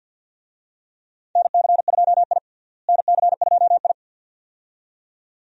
Morse code, heard as a keyed tone, about 50 wpm, 700 Hz. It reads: NC2I NC2I
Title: USA Call Signs - 4 Characters - Repeated 2x - 50wpm